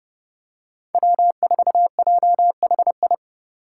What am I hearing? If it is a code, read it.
Morse code, heard as a keyed tone, about 32 wpm, 700 Hz. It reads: W4JHI
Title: Five Challenging Letter-Number Combinations 30wpm